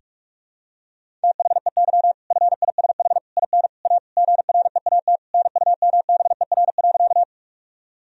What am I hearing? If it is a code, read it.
Morse code, heard as a keyed tone, about 46 wpm, 700 Hz. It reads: THEY FISH IN A GREAT NUMBER.
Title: Sentences from Top 1000 Words - Review Part 1 - 45wpm